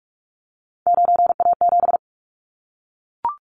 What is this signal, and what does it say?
Morse code, heard as a keyed tone, about 44 wpm, 700 Hz. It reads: 9A7
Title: Callsign Prefixes for Top 100 DXCC - 45wpm